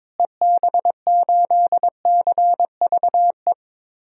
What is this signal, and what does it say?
Morse code, heard as a keyed tone, about 22 wpm, 700 Hz. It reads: EB8CVE
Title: DXCC-100 Call Signs - 6 Characters - Repeated 3x - 22wpm